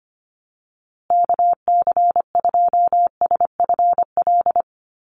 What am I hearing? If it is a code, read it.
Morse code, heard as a keyed tone, about 25 wpm, 700 Hz. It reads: KC2SFL
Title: USA Call Signs Repeated 2x 25wpm